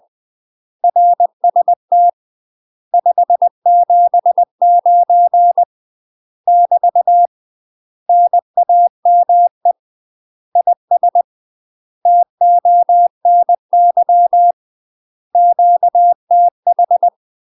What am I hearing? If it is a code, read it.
Morse code, heard as a keyed tone, about 20 wpm, 700 Hz. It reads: RST 579 <BT> NAME IS TONY QTH
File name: UK MASTER QSOs TRAINING / 20WPM / UK MASTER QSOs 0068